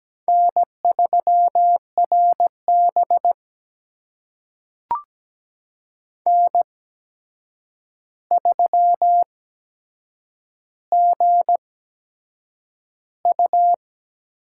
Morse code, heard as a keyed tone, about 17 wpm, spaced as for 7 wpm, 700 Hz. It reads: N3RB N3GU
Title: USA Call Signs - 4 Characters - 8x Character Spacing - 17wpm